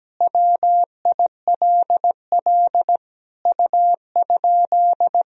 Morse code, heard as a keyed tone, about 17 wpm, 700 Hz. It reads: WILL U?
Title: Sentences from Top 400 Words and Abbreviations 17wpm